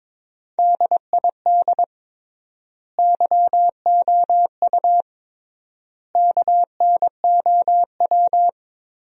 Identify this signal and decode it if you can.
Morse code, heard as a keyed tone, about 23 wpm, 700 Hz. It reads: DID YOU KNOW
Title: 3x Word-Spacing - Sentences from Top 100 Words - 22wpm